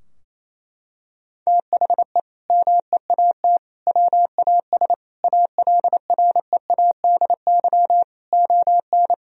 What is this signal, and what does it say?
Morse code, heard as a keyed tone, about 28 wpm, 700 Hz. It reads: THE MEAT WAS ALREADY ON
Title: Sentences from Top 1000 Words - Review Part 3 - 28wpm